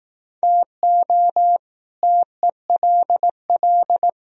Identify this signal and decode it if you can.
Morse code, heard as a keyed tone, about 18 wpm, 700 Hz. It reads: TO TELL
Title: Speed-Racing - Sets of 5 Words 15wpm